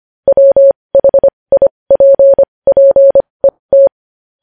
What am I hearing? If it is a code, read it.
Morse code, heard as a keyed tone, about 25 wpm, spaced as for 20 wpm, 550 Hz. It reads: WHIPPET
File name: podcast / News headlines-1x-25x20